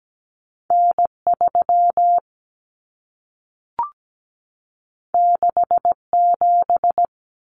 Morse code, heard as a keyed tone, about 17 wpm, 700 Hz. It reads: N3 67
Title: Common Cut Numbers - 2-Digit 17wpm